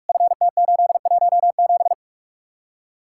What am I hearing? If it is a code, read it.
Morse code, heard as a keyed tone, about 58 wpm, spaced as for 51 wpm, 700 Hz. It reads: FT817
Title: Speed-Racing Common QSO Phrases - 45wpm